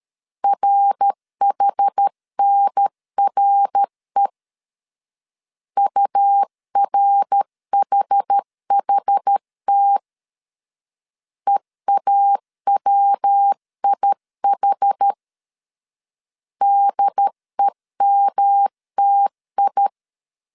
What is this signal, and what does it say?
Morse code, heard as a keyed tone, about 13 wpm, 800 Hz. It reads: RHNRE URHHT EAWIH DEMTI
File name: CW / DET / 1 3 1500C S A 12wpm 2905